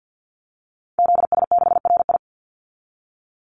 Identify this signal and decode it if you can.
Morse code, heard as a keyed tone, about 50 wpm, 700 Hz. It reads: ZS6RI
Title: Call Signs - 50wpm